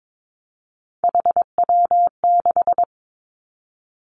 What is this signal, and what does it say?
Morse code, heard as a keyed tone, about 22 wpm, 700 Hz. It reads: HW6
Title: Three Challenging Letter-Number Combinations 22wpm